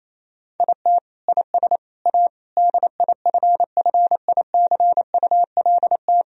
Morse code, heard as a keyed tone, about 28 wpm, 700 Hz. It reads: IT IS A DIFFICULT
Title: Sentences from Top 1000 Words - Review Part 3 - 28wpm